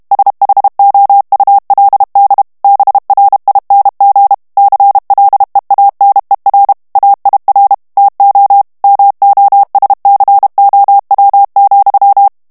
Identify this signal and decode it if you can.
Morse code, heard as a keyed tone, about 32 wpm, 800 Hz. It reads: SHOULD BRING CLEANER AIR TO MOSCOW,
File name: ANGLAIS texte clair 2 vitesse 30 mots minute